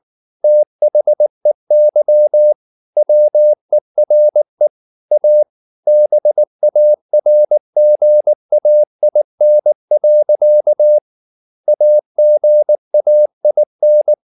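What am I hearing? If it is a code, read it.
Morse code, heard as a keyed tone, about 19 wpm, 600 Hz. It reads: THEY WERE A BARGAIN. AGAIN
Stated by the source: Morse code at 19 wpm from CW med Gnister, reading Ken Follett's novel Never in English.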